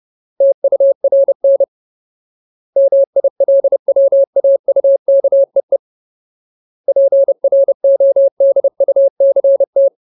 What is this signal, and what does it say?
Morse code, heard as a keyed tone, about 30 wpm, 550 Hz. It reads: TURN MILWAUKEE PRODUCT
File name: Ultimate Word List / 30WPM Ultimate